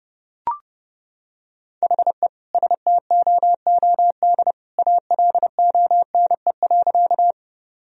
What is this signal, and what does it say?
Morse code, heard as a keyed tone, about 32 wpm, 700 Hz. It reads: HE STOOD ALONE.
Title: Sentences from Top 1000 Words - Review Part 2 - 30wpm